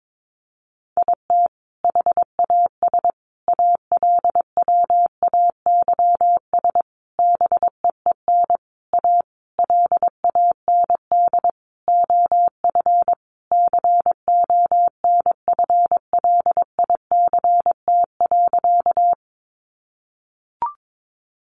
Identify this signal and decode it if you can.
Morse code, heard as a keyed tone, about 22 wpm, 700 Hz. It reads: IT HAS ALWAYS BEEN A LAND OF CONFLICT.
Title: Sentences from Top 1300 Words 22wpm